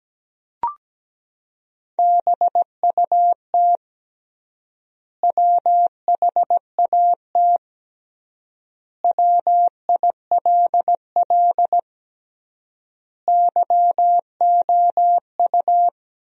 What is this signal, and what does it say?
Morse code, heard as a keyed tone, about 17 wpm, 700 Hz. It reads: BUT WHAT WILL YOU
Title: 3x Word-Spacing - Sentences from Top 100 Words - 17wpm